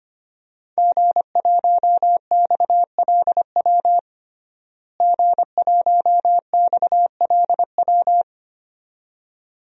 Morse code, heard as a keyed tone, about 25 wpm, 700 Hz. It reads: G1XLW G1XLW
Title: DXCC-100 Call Signs - 5 Characters - Repeated 2x - 25wpm